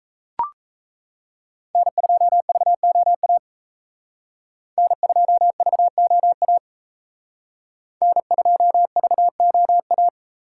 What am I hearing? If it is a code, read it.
Morse code, heard as a keyed tone, about 35 wpm, 700 Hz. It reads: N2VOA N2VOA N2VOA
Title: Speed Racing - USA 1x3 Call Signs 28wpm